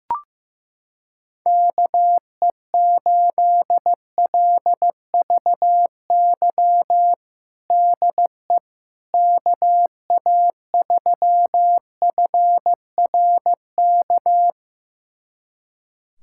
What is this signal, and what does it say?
Morse code, heard as a keyed tone, about 16 wpm, 700 Hz. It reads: KE8LVY DE KA3FRK
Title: USA Callsign DE Callsign - 6 Characters - 15wpm